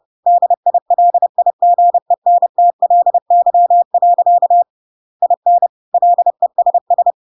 Morse code, heard as a keyed tone, about 30 wpm, 700 Hz. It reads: DILIGENTLY. IN LESS
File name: Pinocchio / Pinocchio ch8 30 30 ik1hge